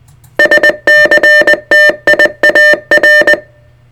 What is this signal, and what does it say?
Morse code, heard as a keyed tone, about 20 wpm, 1800 Hz. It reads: SCTIAR
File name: sartechnology / SCTIAR 600hz 20WPM